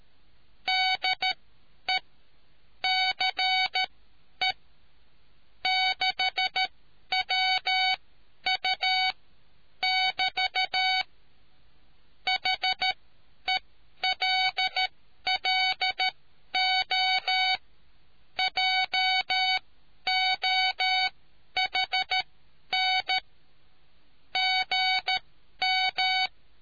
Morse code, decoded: DE CE6WU <BT> HELLO JOHN GM

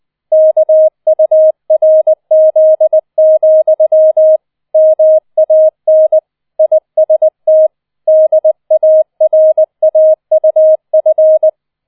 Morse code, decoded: KURZ, MAN IST DARAUF

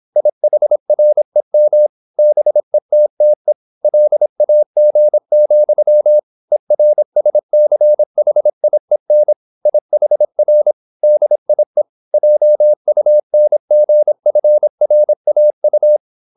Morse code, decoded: IHREM BETTE LAG, ERSCHIEN IHR DIE JUNGFRAU

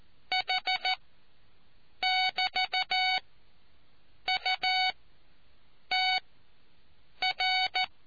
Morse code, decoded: H<BT>UTR

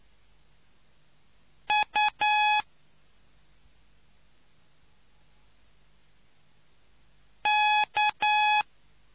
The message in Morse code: U K

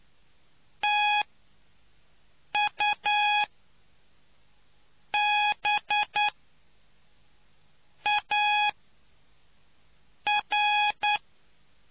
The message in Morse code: TUBAR

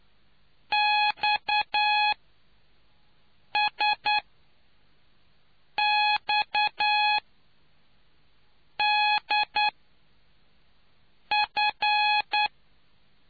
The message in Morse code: XSXDF

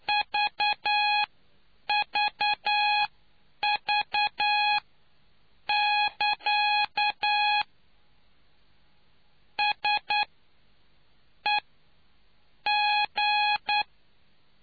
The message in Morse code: VVV <KA> SEG

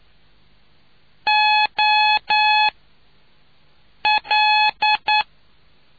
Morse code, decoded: O L

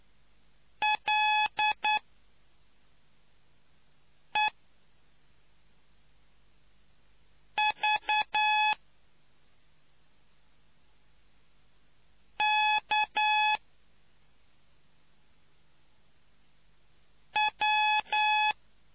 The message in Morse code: LEVKW

